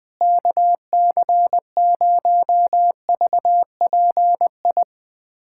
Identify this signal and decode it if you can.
Morse code, heard as a keyed tone, about 20 wpm, 700 Hz. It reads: KC0VPI
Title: USA Callsign DE Callsign 20wpm